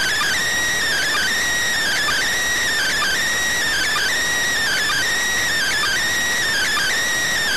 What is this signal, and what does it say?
Emergency Alert System (SAME) tones (header bursts), heard as harsh buzzing bursts. It is heard from 0.9 to 6.9 s.